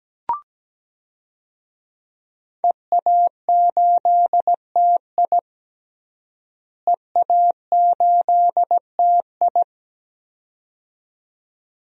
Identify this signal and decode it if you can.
Morse code, heard as a keyed tone, about 17 wpm, 700 Hz. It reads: EA8TI EA8TI
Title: DXCC-100 Call Signs - 5 Characters - Repeated 2x - 17wpm